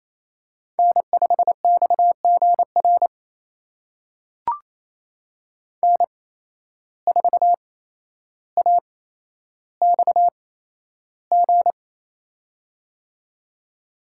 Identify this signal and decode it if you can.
Morse code, heard as a keyed tone, about 28 wpm, spaced as for 12 wpm, 700 Hz. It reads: N5XGR N4AXG
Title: USA Call Signs - 5 Characters - 8x Character Spacing - 28wpm